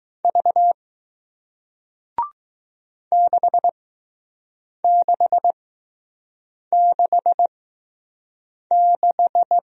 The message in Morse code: V 6666

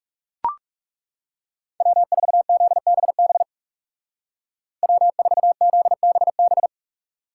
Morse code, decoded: W4ZBB W4ZBB

